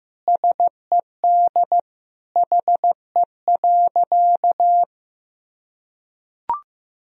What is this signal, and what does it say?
Morse code, heard as a keyed tone, about 15 wpm, 700 Hz. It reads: SED HE.
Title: Sentences from Top 100 Words and Abbreviations 15wpm